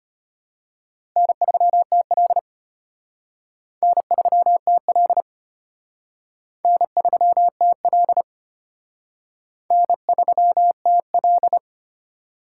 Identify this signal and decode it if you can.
Morse code, heard as a keyed tone, about 32 wpm, 700 Hz. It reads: N3TL N3TL N3TL N3TL